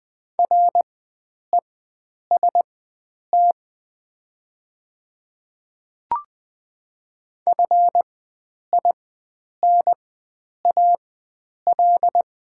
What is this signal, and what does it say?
Morse code, heard as a keyed tone, about 20 wpm, spaced as for 9 wpm, 700 Hz. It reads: REST FINAL
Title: 4x Character Spacing - Top 500 Words - 20wpm